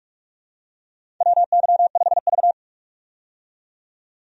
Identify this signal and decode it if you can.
Morse code, heard as a keyed tone, about 46 wpm, 700 Hz. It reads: WY5V
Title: USA Call Signs - 4 Characters - 45wpm